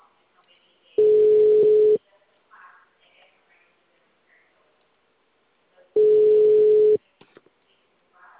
Ringback tone (Europe, 425 Hz).